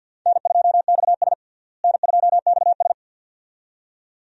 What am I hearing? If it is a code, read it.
Morse code, heard as a keyed tone, about 53 wpm, 700 Hz. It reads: N2XS N2XS
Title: USA Call Signs - 4 Characters - Repeated 2x - 50wpm